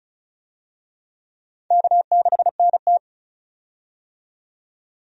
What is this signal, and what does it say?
Morse code, heard as a keyed tone, about 36 wpm, 700 Hz. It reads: K6NT
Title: USA Call Signs - 4 Characters - 35wpm